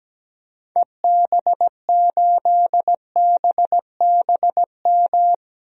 Morse code, heard as a keyed tone, about 17 wpm, 700 Hz. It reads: EB8BBM
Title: DXCC-100 Call Signs - 6 Characters - Repeated 2x - 17wpm